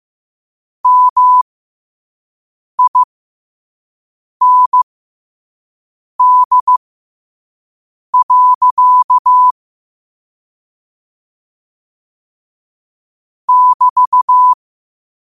Morse code, heard as a keyed tone, about 15 wpm, spaced as for 5 wpm, 1000 Hz. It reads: MIND. <BT>
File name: qotdfiles / 5wpm